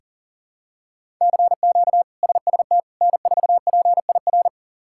Morse code, heard as a keyed tone, about 40 wpm, 700 Hz. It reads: CQ SST N4PIR
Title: K1USN SST QSO Practice - 40wpm